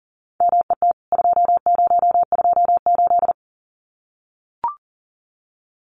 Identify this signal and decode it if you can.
Morse code, heard as a keyed tone, about 39 wpm, 700 Hz. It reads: MET 2028
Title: CWT Exchange 40wpm